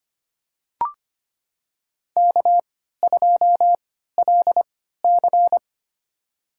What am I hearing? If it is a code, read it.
Morse code, heard as a keyed tone, about 25 wpm, spaced as for 14 wpm, 700 Hz. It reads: K2LC
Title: USA Call Signs - 4 Characters - 3x Character Spacing - 25wpm